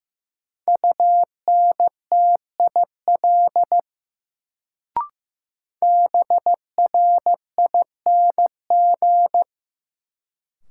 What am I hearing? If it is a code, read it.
Morse code, heard as a keyed tone, about 16 wpm, 700 Hz. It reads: UNTIL BRING
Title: Sentences from Top 500 Words - New Word Drill - 15wpm